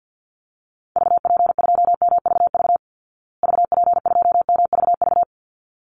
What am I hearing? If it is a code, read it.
Morse code, heard as a keyed tone, about 50 wpm, 700 Hz. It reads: VP2MVV VP2MVV
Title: Call Signs Repeated 3x - 50wpm